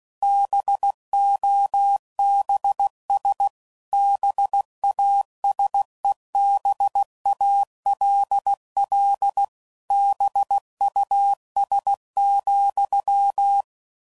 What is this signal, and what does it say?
Morse code, heard as a keyed tone, about 16 wpm, 800 Hz. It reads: BOBS BASEBALL BUS,